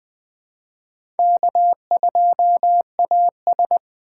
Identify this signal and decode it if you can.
Morse code, heard as a keyed tone, about 20 wpm, 700 Hz. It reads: K2AS